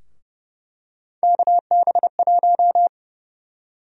Morse code, heard as a keyed tone, about 31 wpm, 700 Hz. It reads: KB1
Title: USA Call Sign Prefixes - Repeated 2x - 30wpm